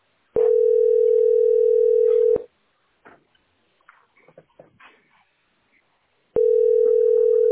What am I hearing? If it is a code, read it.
Ringback tone (North America), 440 + 480 Hz.